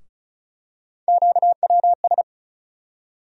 Morse code, heard as a keyed tone, about 36 wpm, 700 Hz. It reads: QWS
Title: USA Call Sign Postfixes - 3 Letters 35wpm